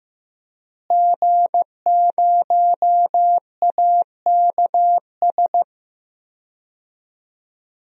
Morse code, heard as a keyed tone, about 15 wpm, 700 Hz. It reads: G0AKS
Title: DXCC-100 Call Signs - 5 Characters - Repeated 2x - 15wpm